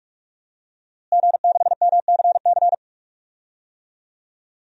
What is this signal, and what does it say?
Morse code, heard as a keyed tone, about 47 wpm, 700 Hz. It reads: G6MCC
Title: DXCC-100 Call Signs - 5 Characters - Repeated 2x - 45wpm